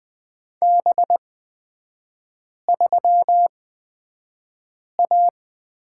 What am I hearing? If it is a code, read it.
Morse code, heard as a keyed tone, about 20 wpm, spaced as for 5 wpm, 700 Hz. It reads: B3A
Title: 250 Lesson 40e - BK - Three Characters 20wpm